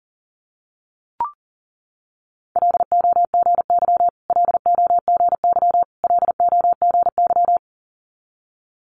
Morse code, heard as a keyed tone, about 38 wpm, 690 Hz. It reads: LOGY LOGY LOGY
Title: Multi-letter Phonemes Repeated 3x 40wpm